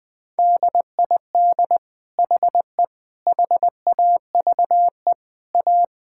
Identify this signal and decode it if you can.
Morse code, heard as a keyed tone, about 20 wpm, 700 Hz. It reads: DID HE HAVE A
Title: Sentences from Top 1000 Words - Review Part 6 - 20wpm